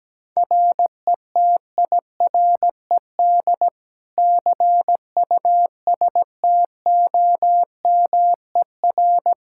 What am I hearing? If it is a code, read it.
Morse code, heard as a keyed tone, about 17 wpm, 700 Hz. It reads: RETIRED CUSTOMER
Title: K1USN SST QSO Practice 17wpm